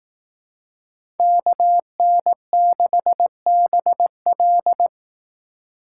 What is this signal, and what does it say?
Morse code, heard as a keyed tone, about 18 wpm, 700 Hz. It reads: KN6BL
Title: Speed Racing - USA Call Signs 15wpm